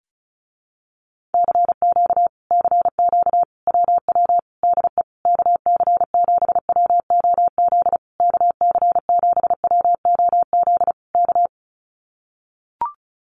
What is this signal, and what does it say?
Morse code, heard as a keyed tone, about 34 wpm, 700 Hz. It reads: CQ CQ WW DE KC7WOZ KC7WOZ K
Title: Common QSO Phrases Encore 35wpm